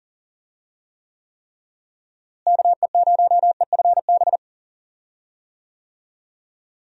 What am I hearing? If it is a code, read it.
Morse code, heard as a keyed tone, about 38 wpm, 700 Hz. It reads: KE0EFB
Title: USA Call Signs - 6 Characters - Repeated 3x - 40wpm